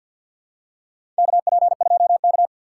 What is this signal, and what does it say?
Morse code, heard as a keyed tone, about 51 wpm, 700 Hz. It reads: KP2K
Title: USA Call Signs - 4 Characters - 3x Character Spacing - 50wpm